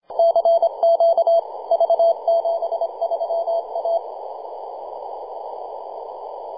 Morse code, heard as a keyed tone, about 27 wpm, 670 Hz. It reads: CQ V73A
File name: posts / 2015-sept-26-2040z-v73a-21-025-cw